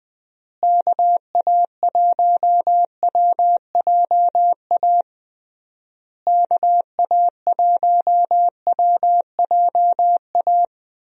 Morse code, heard as a keyed tone, about 21 wpm, 700 Hz. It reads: KA1WJA KA1WJA
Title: USA Call Signs - 6 Characters - Repeated 3x - 20wpm